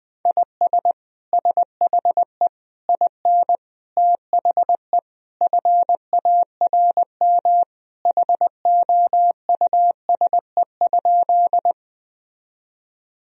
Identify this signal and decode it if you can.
Morse code, heard as a keyed tone, about 20 wpm, 700 Hz. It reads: IS SHE IN THE FARM HOUSE?
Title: Sentences from Top 1000 Words - Review Part 2 - 20wpm